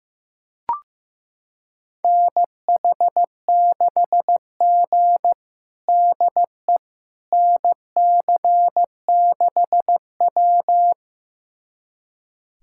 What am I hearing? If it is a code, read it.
Morse code, heard as a keyed tone, about 15 wpm, 700 Hz. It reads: NH6G DE NC6W
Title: USA Callsign DE Callsign - 4 Characters - 15wpm